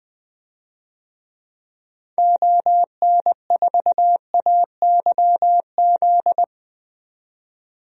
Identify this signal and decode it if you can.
Morse code, heard as a keyed tone, about 20 wpm, 700 Hz. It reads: ON4AYZ
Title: DXCC-100 Call Signs - 6 Characters - 20wpm